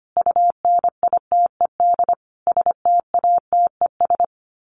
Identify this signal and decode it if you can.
Morse code, heard as a keyed tone, about 25 wpm, 700 Hz. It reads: UNITED STATES
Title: Sets of 5 Words Encore 25wpm